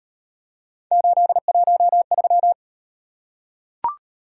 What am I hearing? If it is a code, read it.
Morse code, heard as a keyed tone, about 38 wpm, 700 Hz. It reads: 813